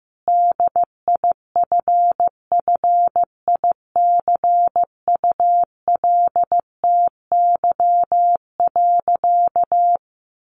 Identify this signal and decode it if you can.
Morse code, heard as a keyed tone, about 15 wpm, 700 Hz. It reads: DIFFICULTY.
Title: Sentences from Top 1500 Words 15wpm